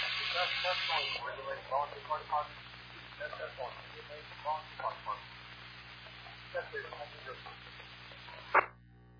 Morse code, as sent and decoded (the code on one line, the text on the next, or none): none